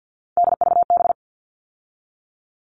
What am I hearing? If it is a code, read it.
Morse code, heard as a keyed tone, about 50 wpm, 700 Hz. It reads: DVB
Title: All Cut Numbers - 3-Digit 50wpm